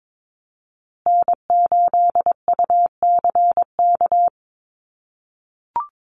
Morse code, heard as a keyed tone, about 22 wpm, 700 Hz. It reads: N8UCK
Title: USA Call Signs Repeated 2x 22wpm